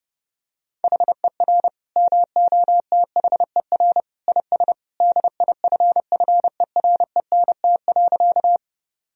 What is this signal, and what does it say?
Morse code, heard as a keyed tone, about 32 wpm, 700 Hz. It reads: HER MOTHER IS DIFFERENT.
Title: Sentences from Top 1000 Words - Review Part 3 - 30wpm